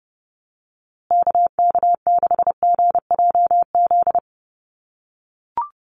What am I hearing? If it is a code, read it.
Morse code, heard as a keyed tone, about 30 wpm, 700 Hz. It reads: KK6GJZ
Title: USA Call Signs Repeated 3x 30wpm